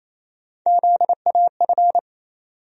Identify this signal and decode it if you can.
Morse code, heard as a keyed tone, about 28 wpm, 700 Hz. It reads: ZAF
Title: USA Call Sign Postfixes - 3 Letters 28wpm